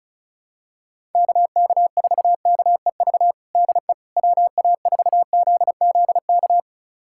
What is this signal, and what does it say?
Morse code, heard as a keyed tone, about 36 wpm, 700 Hz. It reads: KK4KEV DE WA4ZZK
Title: USA Callsign DE Callsign - 6 Characters - 35wpm